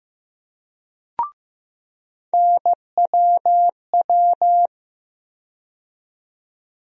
Morse code, heard as a keyed tone, about 15 wpm, 700 Hz. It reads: NWW
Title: USA Call Sign Postfixes - 3 Letters 15wpm